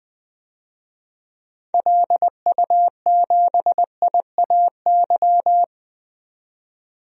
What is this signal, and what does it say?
Morse code, heard as a keyed tone, about 20 wpm, 700 Hz. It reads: LU7IAY
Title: DXCC-100 Call Signs - 6 Characters - 20wpm